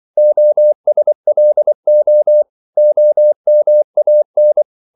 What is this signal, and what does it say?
Morse code, heard as a keyed tone, about 24 wpm, 600 Hz. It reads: OSLO OMAN
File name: CW / Practice / NationalCapitals24